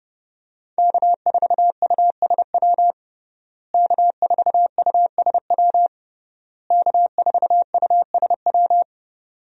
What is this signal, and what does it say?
Morse code, heard as a keyed tone, about 30 wpm, 700 Hz. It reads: K4USW K4USW K4USW
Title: USA Call Signs - 5 Characters - Repeated 3x - 30wpm